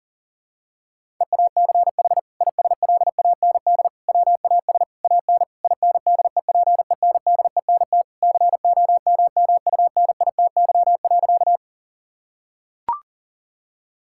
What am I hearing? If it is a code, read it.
Morse code, heard as a keyed tone, about 40 wpm, 700 Hz. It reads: EACH ISLAND WAS AN INDEPENDENT COMMUNITY.
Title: Sentences from Top 1000 Words - Review Part 6 - 40wpm